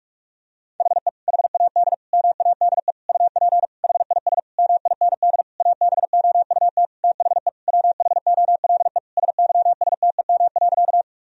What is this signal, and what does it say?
Morse code, heard as a keyed tone, about 41 wpm, 700 Hz. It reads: HE HAD MADE UP HIS MIND ABOUT THE WHOLE SYSTEM.
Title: Sentences from Top 1000 Words - Review Part 3 - 45wpm